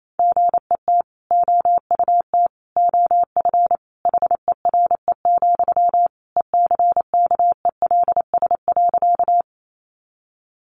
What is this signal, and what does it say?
Morse code, heard as a keyed tone, about 28 wpm, 700 Hz. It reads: GET OUT OF HERE, ECKELS.
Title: A Sound Of Thunder 28wpm